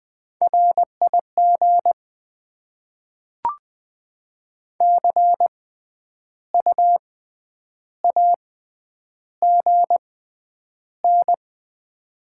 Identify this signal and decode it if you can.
Morse code, heard as a keyed tone, about 20 wpm, spaced as for 9 wpm, 700 Hz. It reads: RIG CUAGN
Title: Top 20 QSO Elements - 6x Character Spacing - 20wpm